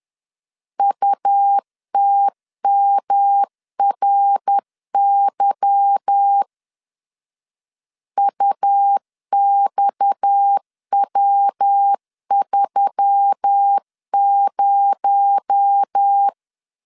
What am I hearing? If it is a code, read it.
Morse code, heard as a keyed tone, about 10 wpm, 800 Hz. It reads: UTMRY UXW30
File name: LEZIONI / 8 YJ45 / 8 Y 10wpm 2615